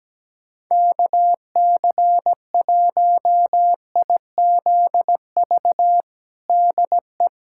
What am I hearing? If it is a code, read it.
Morse code, heard as a keyed tone, about 17 wpm, 700 Hz. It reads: KC1IZV DE
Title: USA Callsign DE Callsign 17wpm